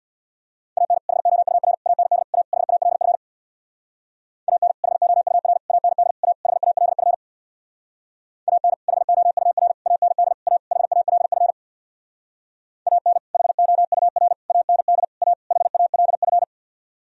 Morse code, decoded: AN HOUR AND A HALF AN HOUR AND A HALF AN HOUR AND A HALF AN HOUR AND A HALF

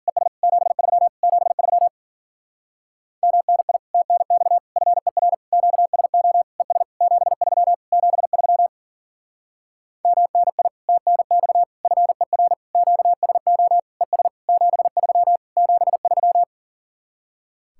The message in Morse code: ES 73 73 MNI TNX FER QSO ES 73 73 MNI TNX FER QSO ES 73 73

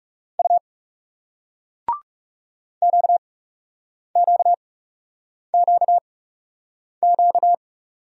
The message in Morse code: U QQQQ

